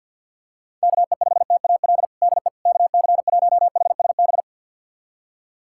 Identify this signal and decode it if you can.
Morse code, heard as a keyed tone, about 46 wpm, 690 Hz. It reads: KE5TAL DE KC1HSB
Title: USA Callsign DE Callsign - 6 Characters - 50wpm